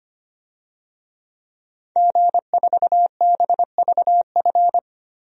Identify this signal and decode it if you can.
Morse code, heard as a keyed tone, about 25 wpm, 700 Hz. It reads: G4BVF
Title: DXCC-100 Call Signs - 5 Characters - Repeated 3x - 25wpm